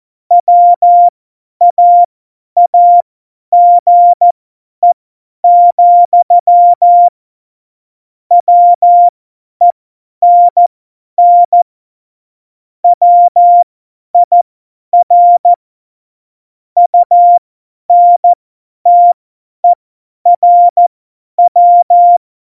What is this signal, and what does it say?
Morse code, heard as a keyed tone, about 14 wpm, spaced as for 10 wpm, 700 Hz. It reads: WAAGE, WENN WIR UNTERW
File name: Morsekurs / Witze / Witz 27. Oktober 2020, 50 F70